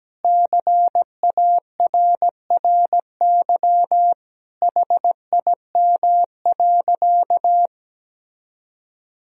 Morse code, heard as a keyed tone, about 17 wpm, 700 Hz. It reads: CARRY HIM.
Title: Sentences from Top 1000 Words - Review Part 1 - 17wpm